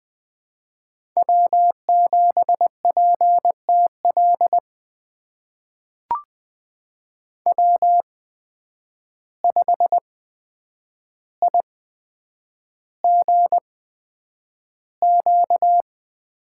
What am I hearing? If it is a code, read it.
Morse code, heard as a keyed tone, about 21 wpm, spaced as for 9 wpm, 700 Hz. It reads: W7PTL W5IGQ
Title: USA Call Signs - 8x Character Spacing - 20wpm